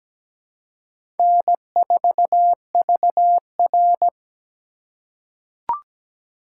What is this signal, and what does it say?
Morse code, heard as a keyed tone, about 17 wpm, 700 Hz. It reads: N4VR
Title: USA Call Signs - 4 Characters - 2x Character Spacing - 17wpm